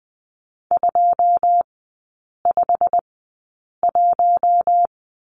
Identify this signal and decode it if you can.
Morse code, heard as a keyed tone, about 20 wpm, spaced as for 8 wpm, 700 Hz. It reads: 251
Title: 087 Lesson 15e - Number 5 - Three Characters 20wpm